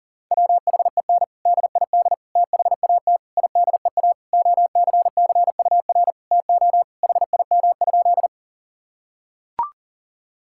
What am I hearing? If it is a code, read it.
Morse code, heard as a keyed tone, about 40 wpm, 700 Hz. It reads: WHEN DID THAT IDEA OCCUR TO HIM?
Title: Sentences from Top 1000 Words - Review Part 6 - 40wpm